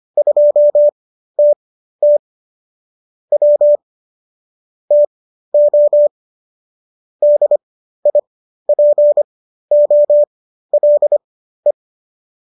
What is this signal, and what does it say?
Morse code, heard as a keyed tone, about 25 wpm, spaced as for 13 wpm, 590 Hz. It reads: 2TT W TO DIPOLE